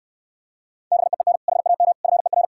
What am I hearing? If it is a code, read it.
Morse code, heard as a keyed tone, about 69 wpm, 700 Hz. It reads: BEEN HAM FER